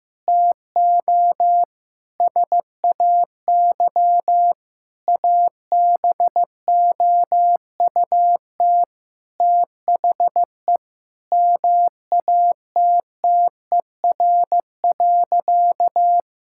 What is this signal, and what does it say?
Morse code, heard as a keyed tone, about 16 wpm, 700 Hz. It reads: TO SAY ABOUT THE MATTER.
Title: Sentences from Top 1000 Words - Review Part 5 - 15wpm